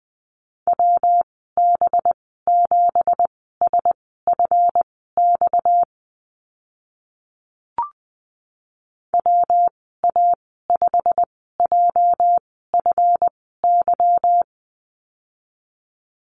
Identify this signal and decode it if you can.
Morse code, heard as a keyed tone, about 20 wpm, spaced as for 14 wpm, 700 Hz. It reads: WB7SFX WA5JFY
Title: 245 Lesson 39g - Letter z - Call Signs 20wpm